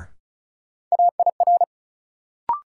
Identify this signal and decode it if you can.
Morse code, heard as a keyed tone, about 34 wpm, 700 Hz. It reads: AIR
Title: Sentences from Top 200 Words - New Word Drill - 35wpm